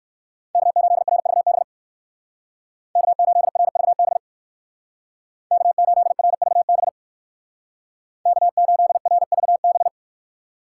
Morse code, heard as a keyed tone, about 55 wpm, 700 Hz. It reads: K8RVB K8RVB K8RVB K8RVB
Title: Speed Racing - USA 1x3 Call Signs 45wpm